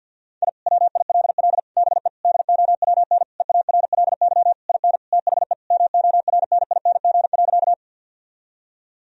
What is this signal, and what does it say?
Morse code, heard as a keyed tone, about 50 wpm, 700 Hz. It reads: I WILL BE DOWN EARLY IN THE MORNING.
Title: Sentences from Top 1000 Words - Review Part 2 - 50wpm